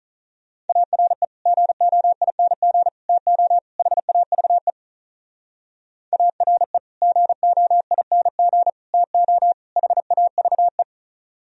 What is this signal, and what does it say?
Morse code, heard as a keyed tone, about 37 wpm, 700 Hz. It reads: ARE GOING TO HAVE ARE GOING TO HAVE
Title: Speed-Racing - Sets of 4 Words 35wpm